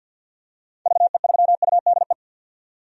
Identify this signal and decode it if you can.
Morse code, heard as a keyed tone, about 50 wpm, 700 Hz. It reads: VE3UDE
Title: DXCC-100 Call Signs - 6 Characters - 50wpm